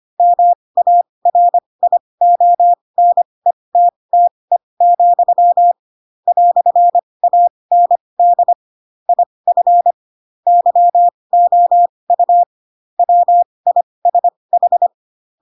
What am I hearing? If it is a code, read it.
Morse code, heard as a keyed tone, about 25 wpm, spaced as for 20 wpm, 700 Hz. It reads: MARIONETTE, "AND IF YOU WISH
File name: Pinocchio / Pinocchio ch4 15 25 ik1hge